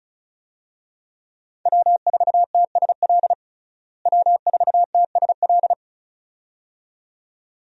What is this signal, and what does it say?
Morse code, heard as a keyed tone, about 34 wpm, 700 Hz. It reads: W4TSL W4TSL
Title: USA Call Signs - 5 Characters - Repeated 2x - 35wpm